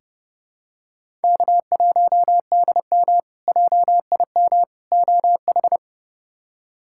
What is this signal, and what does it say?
Morse code, heard as a keyed tone, about 30 wpm, 700 Hz. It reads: K1DM JIM OH
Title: K1USN SST QSO Practice - 30wpm